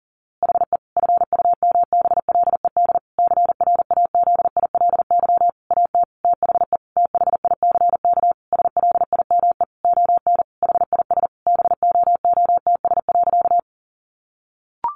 Morse code, heard as a keyed tone, about 37 wpm, 700 Hz. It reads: HE FUMBLED CRAZILY AT THE THICK SLIME ON HIS BOOTS.
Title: A Sound Of Thunder 40wpm